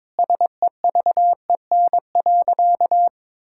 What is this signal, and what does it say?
Morse code, heard as a keyed tone, about 21 wpm, 700 Hz. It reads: SEVEN.
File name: practice / 22wpm / Sentences with 4-Word Phrases - 22wpm